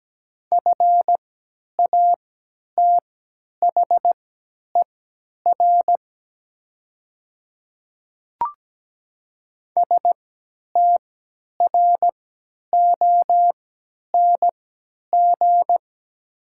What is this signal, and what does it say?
Morse code, heard as a keyed tone, about 17 wpm, spaced as for 10 wpm, 700 Hz. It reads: FATHER STRONG
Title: 3x Character Spacing - Top 500 Words - 17wpm